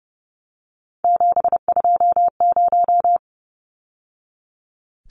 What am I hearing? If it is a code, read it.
Morse code, heard as a keyed tone, about 30 wpm, 700 Hz. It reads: 720
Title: Numbers - 3-Digit 30wpm